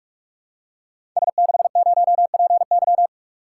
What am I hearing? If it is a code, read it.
Morse code, heard as a keyed tone, about 44 wpm, 700 Hz. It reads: S60PY
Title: Code Groups - 5 Letter:Numbers - 45wpm